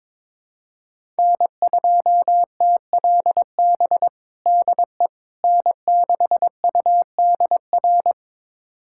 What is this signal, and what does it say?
Morse code, heard as a keyed tone, about 22 wpm, 700 Hz. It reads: N2TLB DE N6UDR
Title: USA Callsign DE Callsign - 5 Characters - 22wpm